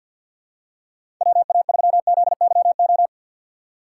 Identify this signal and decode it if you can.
Morse code, heard as a keyed tone, about 53 wpm, 700 Hz. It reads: WA3ZYO
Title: USA Call Signs - 6 Characters - 3x Character Spacing - 50wpm